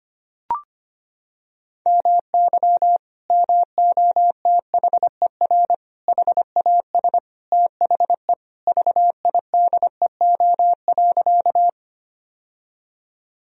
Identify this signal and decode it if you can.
Morse code, heard as a keyed tone, about 25 wpm, 700 Hz. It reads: MY MOTHER HAS THE VIDEO.
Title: Sentences from Top 1000 Words - Review Part 4 - 25wpm